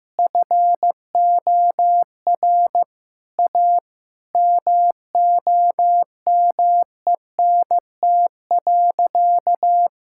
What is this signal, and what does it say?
Morse code, heard as a keyed tone, about 15 wpm, 700 Hz. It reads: FOR A MOMENT.
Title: Sentences from Top 1600 Words 15wpm